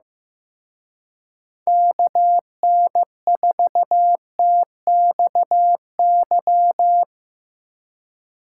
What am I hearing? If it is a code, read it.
Morse code, heard as a keyed tone, about 17 wpm, 700 Hz. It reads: KN4TXY
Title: Speed Racing - USA 2x3 Call Signs 15wpm